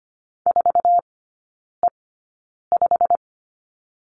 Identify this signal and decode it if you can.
Morse code, heard as a keyed tone, about 25 wpm, spaced as for 9 wpm, 700 Hz. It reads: 4E5
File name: Lesson 15e - Number 5 - Three Characters 25wpm